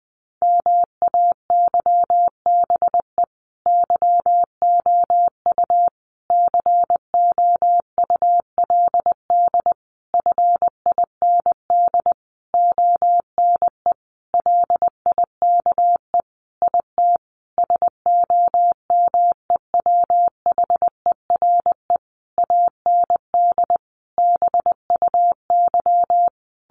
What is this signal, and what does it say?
Morse code, heard as a keyed tone, about 20 wpm, 700 Hz. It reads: MAYBE YOU COULD FIND ONE LIKE IT SOMEWHERE AND BUY
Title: Sentences from Top 1400 Words 20wpm